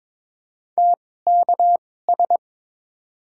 Morse code, heard as a keyed tone, about 22 wpm, spaced as for 16 wpm, 700 Hz. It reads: TKS
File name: Top 90 QSO Elements - 2x Character Spacing - 22wpm